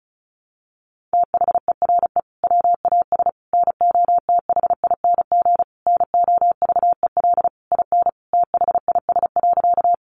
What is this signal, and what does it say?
Morse code, heard as a keyed tone, about 34 wpm, 700 Hz. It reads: THERE WAS NOTHING NOVEL IN THIS.
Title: Sentences from Top 1300 Words 35wpm